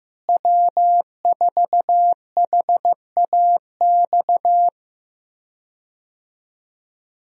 Morse code, heard as a keyed tone, about 15 wpm, 700 Hz. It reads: W4HAX
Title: USA Call Signs - 5 Characters - Repeated 3x - 15wpm